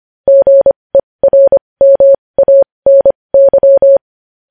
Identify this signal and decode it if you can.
Morse code, heard as a keyed tone, about 25 wpm, spaced as for 20 wpm, 550 Hz. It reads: GERMANY
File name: podcast / News headlines-1x-25x20